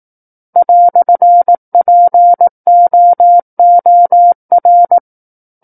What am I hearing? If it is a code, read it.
Morse code, heard as a keyed tone, about 18 wpm, 700 Hz. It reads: "POOR
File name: Book One / 09-Chapter 9 - 18 WPM